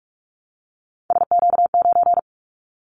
Morse code, heard as a keyed tone, about 46 wpm, 700 Hz. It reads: SQ9
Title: Callsign Prefixes for Top 25 DXCC - 45wpm